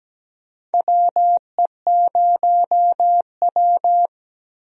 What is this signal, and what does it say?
Morse code, heard as a keyed tone, about 17 wpm, 700 Hz. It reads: WE0W
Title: USA Call Signs - 4 Characters - Repeated 3x - 17wpm